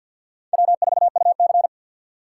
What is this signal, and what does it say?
Morse code, heard as a keyed tone, about 48 wpm, 700 Hz. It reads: W4UC
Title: USA Call Signs - 4 Characters - 6x Character Spacing - 50wpm